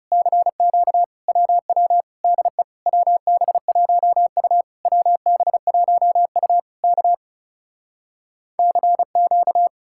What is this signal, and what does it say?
Morse code, heard as a keyed tone, about 40 wpm, 700 Hz. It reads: CQ WW DE WB1U WB1U K CQ